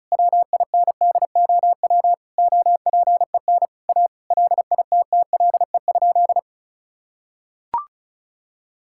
Morse code, heard as a keyed tone, about 34 wpm, 700 Hz. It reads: WINDOW OPEN A LITTLE?
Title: Sentences from Top 1600 Words 35wpm